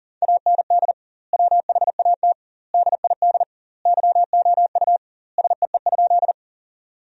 Morse code, heard as a keyed tone, about 37 wpm, 700 Hz. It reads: AND WHAT DID YOU SEE?
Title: 2x Word-Spacing - Sentences from Top 100 Words - 40wpm